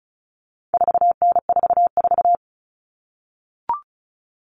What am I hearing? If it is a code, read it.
Morse code, heard as a keyed tone, about 34 wpm, 700 Hz. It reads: 4N44
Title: All Cut Numbers - 4-Digit 35wpm